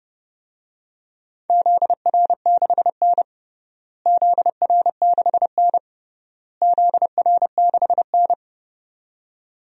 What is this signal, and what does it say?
Morse code, heard as a keyed tone, about 30 wpm, 700 Hz. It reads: ZR6N ZR6N ZR6N
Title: DXCC-100 Call Signs - 4 Characters - Repeated 3x - 30wpm